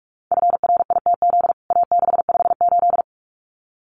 Morse code, heard as a keyed tone, about 44 wpm, 700 Hz. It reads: FRITZ A658
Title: CWT Exchange 45wpm